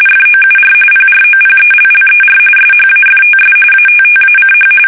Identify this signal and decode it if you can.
RTTY, about 45.45 baud. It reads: FOUR PROPELLER SHAFTS ATTACH
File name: fsk850 / battleshipnj